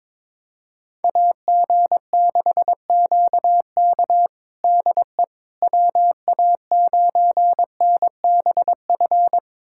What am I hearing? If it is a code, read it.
Morse code, heard as a keyed tone, about 22 wpm, 700 Hz. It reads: AG6QK DE WA9NBF
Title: USA Callsign DE Callsign 22wpm